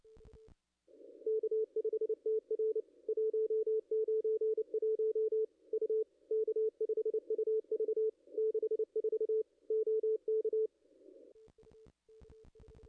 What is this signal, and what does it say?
Morse code, heard as a keyed tone, about 28 wpm, 440 Hz. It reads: K5TR 191 U K5UV 64 OK
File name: contests / ss-cw-k5tr-op-n5rz -0200